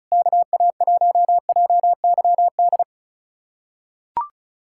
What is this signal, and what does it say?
Morse code, heard as a keyed tone, about 36 wpm, 700 Hz. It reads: KA1JYD
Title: USA Callsign DE Callsign 35wpm